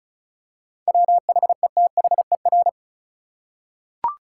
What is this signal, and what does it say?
Morse code, heard as a keyed tone, about 34 wpm, 700 Hz. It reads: WHETHER
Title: Seven Letter Words - 35wpm